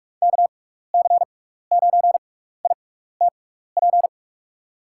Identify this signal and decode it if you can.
Morse code, heard as a keyed tone, about 46 wpm, spaced as for 15 wpm, 700 Hz. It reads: KC9ITP
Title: USA Call Signs - 6x Character Spacing - 45wpm